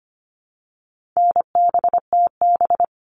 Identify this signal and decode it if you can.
Morse code, heard as a keyed tone, about 25 wpm, 700 Hz. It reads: NBTB
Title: All Cut Numbers - 4-Digit 25wpm